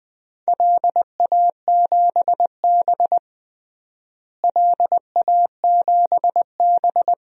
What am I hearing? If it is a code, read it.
Morse code, heard as a keyed tone, about 20 wpm, 700 Hz. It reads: LA7B LA7B
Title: DXCC-100 Call Signs - 4 Characters - Repeated 2x - 20wpm